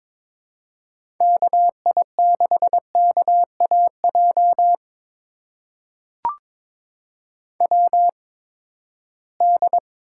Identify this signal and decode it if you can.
Morse code, heard as a keyed tone, about 22 wpm, spaced as for 15 wpm, 700 Hz. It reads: KI6KAJ WD
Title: USA Call Signs - 6 Characters - 8x Character Spacing - 22wpm